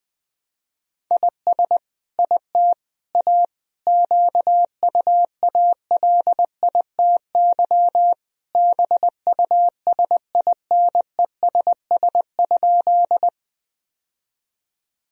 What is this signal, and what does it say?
Morse code, heard as a keyed tone, about 21 wpm, 700 Hz. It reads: IS IT A QUALITY BUSINESS?
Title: Sentences from Top 1000 Words - Review Part 4 - 20wpm